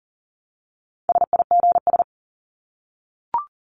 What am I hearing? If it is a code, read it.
Morse code, heard as a keyed tone, about 40 wpm, 700 Hz. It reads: SIGS
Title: Abbreviations 40wpm